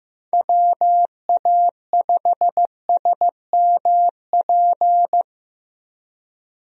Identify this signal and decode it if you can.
Morse code, heard as a keyed tone, about 15 wpm, 700 Hz. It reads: WA5SMP
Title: USA Call Signs - 6 Characters - 15wpm